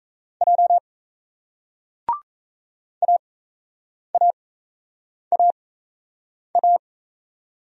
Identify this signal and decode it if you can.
Morse code, heard as a keyed tone, about 35 wpm, spaced as for 8 wpm, 700 Hz. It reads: J AAAA
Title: Speed-Racing Single Letters 28wpm